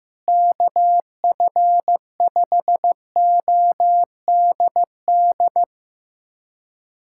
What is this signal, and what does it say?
Morse code, heard as a keyed tone, about 15 wpm, 700 Hz. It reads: KF5ODD
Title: USA Callsign DE Callsign 15wpm